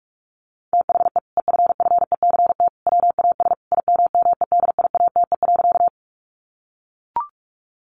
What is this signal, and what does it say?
Morse code, heard as a keyed tone, about 44 wpm, 700 Hz. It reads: THE EFFECT WAS IMMEDIATE.
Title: Sentences from Top 1200 Words 45wpm